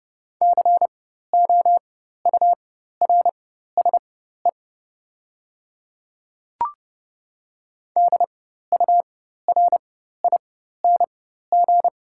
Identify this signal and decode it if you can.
Morse code, heard as a keyed tone, about 30 wpm, spaced as for 14 wpm, 700 Hz. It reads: COURSE DURING
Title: 4x Character Spacing - Top 500 Words - 30wpm